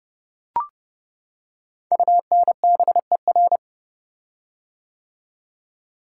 Morse code, heard as a keyed tone, about 30 wpm, 690 Hz. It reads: UNBER
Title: Code Groups - 5 Letters - 30wpm